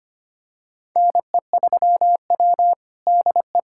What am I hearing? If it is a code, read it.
Morse code, heard as a keyed tone, about 25 wpm, 700 Hz. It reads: NE3W DE
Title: USA Callsign DE Callsign - 4 Characters - 25wpm